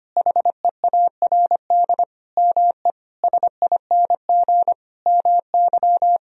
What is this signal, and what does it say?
Morse code, heard as a keyed tone, about 25 wpm, 700 Hz. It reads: HEARD ME SING MY
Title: Sentences from Top 1000 Words - Review Part 2 - 25wpm